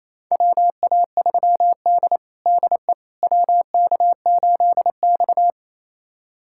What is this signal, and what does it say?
Morse code, heard as a keyed tone, about 28 wpm, 700 Hz. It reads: WA3D DE WK8X
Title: USA Callsign DE Callsign - 4 Characters - 28wpm